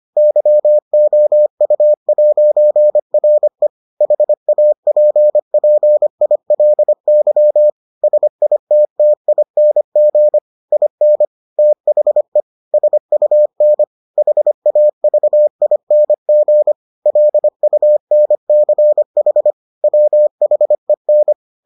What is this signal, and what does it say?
Morse code, heard as a keyed tone, about 25 wpm, 600 Hz. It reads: YOU'RE HAPPILY SITTING IN THE SUN HAVING LUNCH WHEN